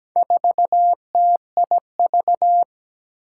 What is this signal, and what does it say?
Morse code, heard as a keyed tone, about 17 wpm, 700 Hz. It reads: 4TIV
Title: Four Challenging Letter-Number Combinations 17wpm